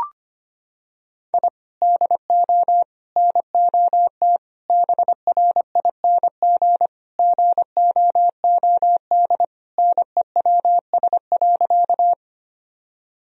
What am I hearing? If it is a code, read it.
Morse code, heard as a keyed tone, about 25 wpm, 700 Hz. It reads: I DO NOT BRING GOOD NEWS.
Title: Sentences from Top 1000 Words - Review Part 3 - 25wpm